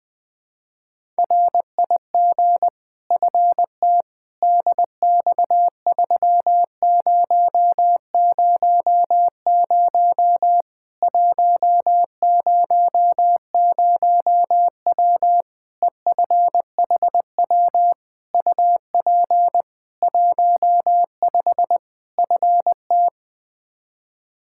Morse code, heard as a keyed tone, about 21 wpm, 700 Hz. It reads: RIG FT DX3000 100W EFHW UP 15 FT
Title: Practice QSO - Exchange 2 - Info - 20wpm